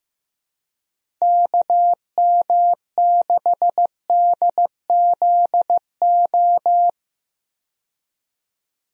Morse code, heard as a keyed tone, about 15 wpm, 700 Hz. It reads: KM6DZO
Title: USA Call Signs - 6 Characters - 15wpm